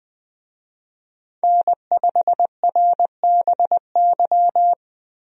Morse code, heard as a keyed tone, about 20 wpm, 700 Hz. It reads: N5RBY
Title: USA Call Signs - 5 Characters - Repeated 3x - 20wpm